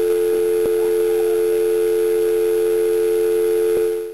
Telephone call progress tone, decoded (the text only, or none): dial tone